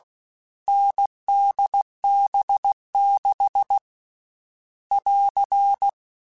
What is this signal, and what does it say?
Morse code, heard as a keyed tone, about 16 wpm, 800 Hz. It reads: NDB6 <AR>
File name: Difficult Combos / Difficult Combos-mp3s / NDB NDB6 @15